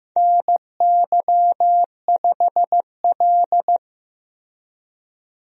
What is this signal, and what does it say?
Morse code, heard as a keyed tone, about 15 wpm, 700 Hz. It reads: NY5L
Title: USA Call Signs - 4 Characters - Repeated 2x - 15wpm